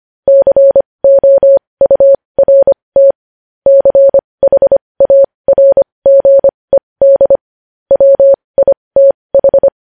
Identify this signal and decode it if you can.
Morse code, heard as a keyed tone, about 25 wpm, spaced as for 20 wpm, 550 Hz. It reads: COURT CHARGED WITH
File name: podcast / News headlines-1x-25x20